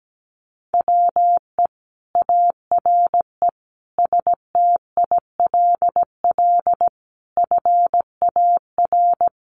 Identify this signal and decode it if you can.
Morse code, heard as a keyed tone, about 17 wpm, 700 Hz. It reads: WE ARE STILL FAR
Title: Sentences from Top 1300 Words 17wpm